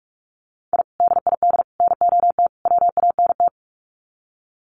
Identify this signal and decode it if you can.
Morse code, heard as a keyed tone, about 44 wpm, 700 Hz. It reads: I DID NOT WANT
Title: Sets of 4 Words 45wpm